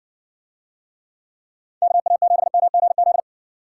Morse code, heard as a keyed tone, about 54 wpm, 700 Hz. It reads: KA7MGZ